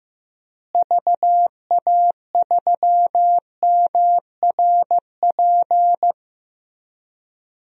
Morse code, heard as a keyed tone, about 15 wpm, 700 Hz. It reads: VA3MRP
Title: DXCC-100 Call Signs - 6 Characters - Repeated 2x - 15wpm